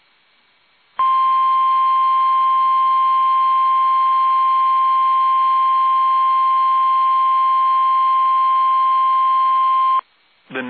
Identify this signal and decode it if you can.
Emergency Alert System (SAME) tones (the 1050 Hz attention tone), with speech. It is heard from 0.9 to 10.1 s.
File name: eas / KUSO / KUSO